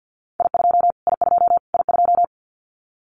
Morse code, heard as a keyed tone, about 50 wpm, 700 Hz. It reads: I2 I2 I2
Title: Callsign Prefixes for Top 25 DXCC - Repeated 3x - 50wpm